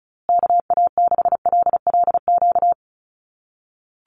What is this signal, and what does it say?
Morse code, heard as a keyed tone, about 34 wpm, 700 Hz. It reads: KA6LLQ
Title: USA Call Signs Repeated 3x 35wpm